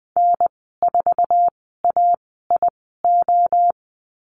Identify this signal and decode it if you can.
Morse code, heard as a keyed tone, about 20 wpm, spaced as for 14 wpm, 700 Hz. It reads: N4AIO
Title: 076 Lesson 13g - Letter L - Call Signs 20wpm